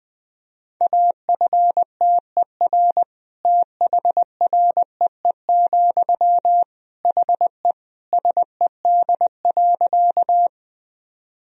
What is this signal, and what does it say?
Morse code, heard as a keyed tone, about 20 wpm, 700 Hz. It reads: AFTER THREE, HE SED.
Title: Sentences from Top 200 Words and Abbreviations 20wpm